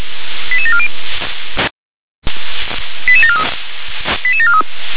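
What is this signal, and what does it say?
Five-tone selective call (ZVEI), heard as quick runs of five tones, tones of about 80 ms. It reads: 88740 88743 88743
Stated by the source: ZVEI five-tone selective call received by SDR.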